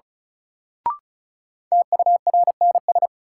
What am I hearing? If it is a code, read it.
Morse code, heard as a keyed tone, about 36 wpm, 700 Hz. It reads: TURNS
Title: Sets of 5 Words Encore - New Word Drill - 35wpm